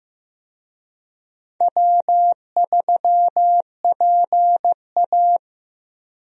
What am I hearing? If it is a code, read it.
Morse code, heard as a keyed tone, about 15 wpm, 700 Hz. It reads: W3PA
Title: USA Call Signs - 4 Characters - Repeated 3x - 15wpm